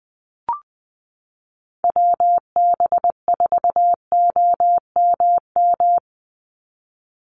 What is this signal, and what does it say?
Morse code, heard as a keyed tone, about 20 wpm, 700 Hz. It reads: WB4OMM
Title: Call Signs Repeated 2x - 20wpm